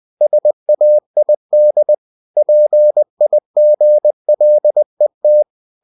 Morse code, heard as a keyed tone, about 20 wpm, 600 Hz. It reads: SAID PIGLET
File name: cw-books / winnie-the-pooh / winnie-20wpm-page-0079